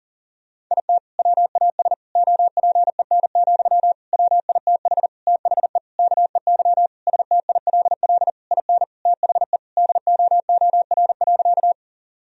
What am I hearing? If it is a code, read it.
Morse code, heard as a keyed tone, about 40 wpm, 700 Hz. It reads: IT WAS OPEN, WITH THE KEY STILL IN THE DOOR.
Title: Sentences from Top 1000 Words - Review Part 4 - 40wpm